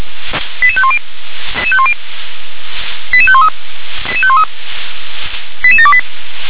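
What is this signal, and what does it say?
Five-tone selective call (ZVEI), heard as quick runs of five tones, tones of about 70 ms. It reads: 88510 88513 88728
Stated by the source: ZVEI five-tone call 88730 received by SDR.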